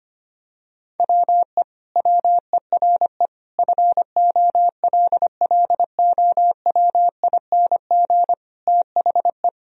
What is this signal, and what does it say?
Morse code, heard as a keyed tone, about 25 wpm, 700 Hz. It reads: WE WERE FOLLOWING THE
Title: Sentences from Top 1000 Words - Review Part 4 - 25wpm